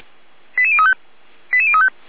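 Five-tone selective call (ZVEI), heard as quick runs of five tones, tones of about 70 ms. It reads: 88035 88035
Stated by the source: ZVEI five-tone call received by SDR.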